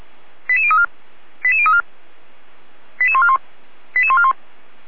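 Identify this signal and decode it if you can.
Five-tone selective call (ZVEI), heard as quick runs of five tones, tones of about 70 ms. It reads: 88034 88034 88141 88141
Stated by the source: ZVEI five-tone selective call received on an SDR, from the Feuerwehr Orlamünde audio archive.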